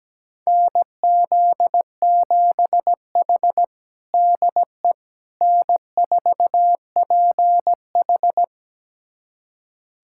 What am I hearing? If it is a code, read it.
Morse code, heard as a keyed tone, about 17 wpm, 700 Hz. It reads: NZ7H DE N4PH
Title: USA Callsign DE Callsign - 4 Characters - 17wpm